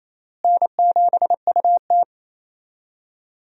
Morse code, heard as a keyed tone, about 28 wpm, 700 Hz. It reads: N7UT
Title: USA Call Signs - 4 Characters - Repeated 2x - 28wpm